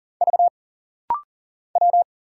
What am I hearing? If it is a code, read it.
Morse code, heard as a keyed tone, about 40 wpm, 700 Hz. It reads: V W
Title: Single Letters - Rapid-Fire 40wpm